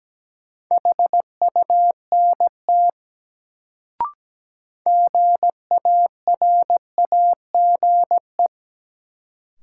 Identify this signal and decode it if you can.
Morse code, heard as a keyed tone, about 17 wpm, 700 Hz. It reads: HUNT GARAGE
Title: Sentences from Top 1600 Words - New Word Drill - 17wpm